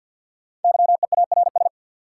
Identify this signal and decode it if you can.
Morse code, heard as a keyed tone, about 53 wpm, 700 Hz. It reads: YEARS
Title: Sentences from Top 200 Words - New Word Drill - 50wpm